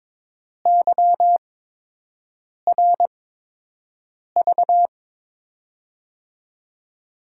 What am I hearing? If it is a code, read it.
Morse code, heard as a keyed tone, about 21 wpm, spaced as for 6 wpm, 700 Hz. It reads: YRV
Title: USA Call Signs - 6 Characters - 8x Character Spacing - 22wpm